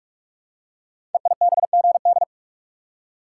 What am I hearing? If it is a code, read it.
Morse code, heard as a keyed tone, about 44 wpm, 690 Hz. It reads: EIBGD
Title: Code Groups - 5 Letters - 45wpm